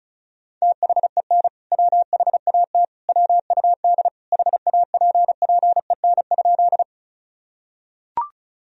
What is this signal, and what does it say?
Morse code, heard as a keyed tone, about 36 wpm, 700 Hz. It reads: THEN WHAT WUD HAPPEN?
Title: Sentences from Top 400 Words and Abbreviations 35wpm